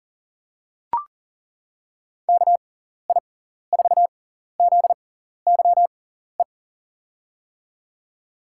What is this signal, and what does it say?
Morse code, heard as a keyed tone, about 39 wpm, spaced as for 14 wpm, 700 Hz. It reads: KI4ZYE
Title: USA Call Signs - 6 Characters - 6x Character Spacing - 40wpm